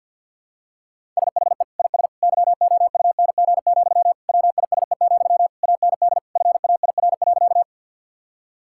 Morse code, heard as a keyed tone, about 50 wpm, 700 Hz. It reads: SHE IS YOUNG, WISE, AND FAIR.
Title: Sentences from Top 1600 Words 50wpm